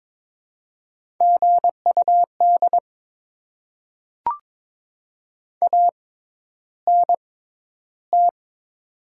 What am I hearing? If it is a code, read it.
Morse code, heard as a keyed tone, about 21 wpm, spaced as for 11 wpm, 700 Hz. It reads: GUD ANT
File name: Top 20 QSO Elements - 6x Character Spacing - 22wpm